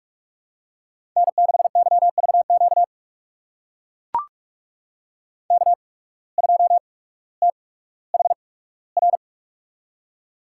Morse code, heard as a keyed tone, about 44 wpm, spaced as for 19 wpm, 700 Hz. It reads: N6YVQ K2THR
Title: USA Call Signs - 5 Characters - 8x Character Spacing - 45wpm